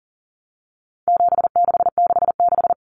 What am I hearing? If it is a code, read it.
Morse code, heard as a keyed tone, about 40 wpm, 700 Hz. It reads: 7666
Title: Numbers - 4-Digit 40wpm